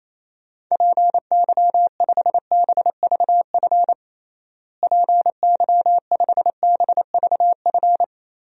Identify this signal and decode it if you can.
Morse code, heard as a keyed tone, about 28 wpm, 700 Hz. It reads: PY5BVF PY5BVF
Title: DXCC-100 Call Signs - 6 Characters - Repeated 3x - 28wpm